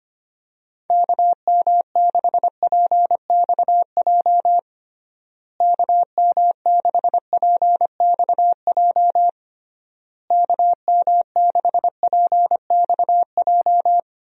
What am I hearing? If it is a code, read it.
Morse code, heard as a keyed tone, about 25 wpm, 700 Hz. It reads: KM6PXJ KM6PXJ KM6PXJ
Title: USA Call Signs - 6 Characters - Repeated 3x - 25wpm